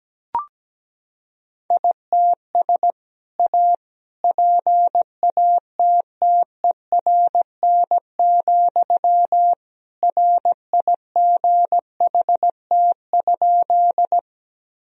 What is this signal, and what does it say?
Morse code, heard as a keyed tone, about 17 wpm, 700 Hz. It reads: ITS A PATTERN, RIGHT?
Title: Sentences from Top 1000 Words - Review Part 2 - 17wpm